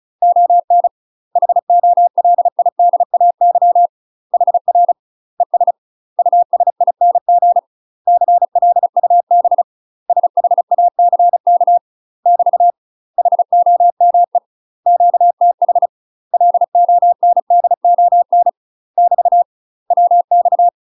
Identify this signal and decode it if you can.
Morse code, heard as a keyed tone, about 34 wpm, 700 Hz. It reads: ON HOLIDAY HR ES USING CLUB SHACK <BT> HOME QTH LONDON <BT> WX
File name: UK MASTER QSOs TRAINING / 35WPM / UK MASTER QSOs 0001